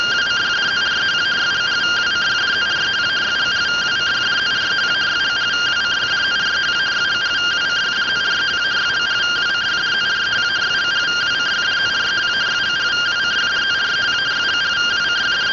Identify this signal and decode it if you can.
Two-tone teleprinter signal, about 170 Hz shift.